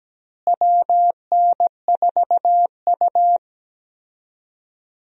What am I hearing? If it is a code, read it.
Morse code, heard as a keyed tone, about 17 wpm, 700 Hz. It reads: WN4U
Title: USA Call Signs - 4 Characters - Repeated 2x - 17wpm